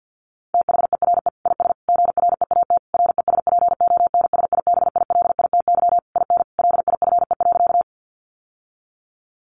Morse code, heard as a keyed tone, about 50 wpm, 700 Hz. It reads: THERE IS GREAT RESPONSIBILITY IN LIFE.
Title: Sentences from Top 1200 Words 50wpm